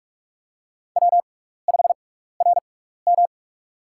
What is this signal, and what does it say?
Morse code, heard as a keyed tone, about 46 wpm, spaced as for 15 wpm, 700 Hz. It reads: W5RM
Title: USA Call Signs - 4 Characters - 6x Character Spacing - 45wpm